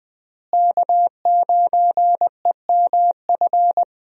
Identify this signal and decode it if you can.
Morse code, heard as a keyed tone, about 20 wpm, 700 Hz. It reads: K9EMF
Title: USA Call Signs - 5 Characters - Repeated 2x - 20wpm